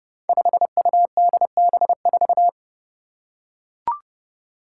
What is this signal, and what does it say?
Morse code, heard as a keyed tone, about 31 wpm, 700 Hz. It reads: 5UDB4
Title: Five Challenging Letter-Number Combinations 30wpm